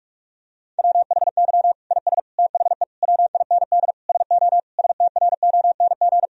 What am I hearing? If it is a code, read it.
Morse code, heard as a keyed tone, about 46 wpm, 700 Hz. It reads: WHY IS THE WIND SO STRONG
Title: Sentences from Top 1000 Words - Review Part 2 - 45wpm